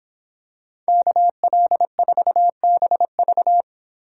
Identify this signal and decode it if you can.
Morse code, heard as a keyed tone, about 26 wpm, 700 Hz. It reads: KL4BV